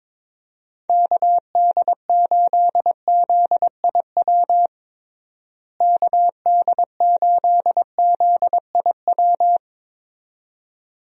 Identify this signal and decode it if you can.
Morse code, heard as a keyed tone, about 22 wpm, 700 Hz. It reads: KD8ZIW KD8ZIW
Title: USA Call Signs - 6 Characters - Repeated 2x - 22wpm